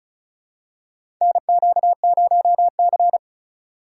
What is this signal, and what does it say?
Morse code, heard as a keyed tone, about 36 wpm, 700 Hz. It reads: NQ0C
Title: USA Call Signs - 4 Characters - 2x Character Spacing - 35wpm